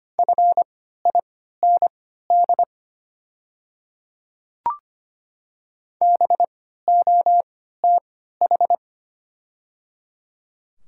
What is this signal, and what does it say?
Morse code, heard as a keyed tone, about 25 wpm, spaced as for 14 wpm, 700 Hz. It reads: FIND BOTH
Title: 3x Character Spacing - Top 500 Words - 25wpm